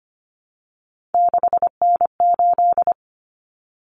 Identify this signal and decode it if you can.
Morse code, heard as a keyed tone, about 25 wpm, 700 Hz. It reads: 6N8
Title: Common Cut Numbers - 3-Digit 25wpm